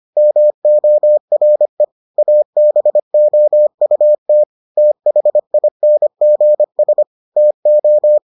Morse code, heard as a keyed tone, about 25 wpm, 600 Hz. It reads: MORE ABOUT THINGS TO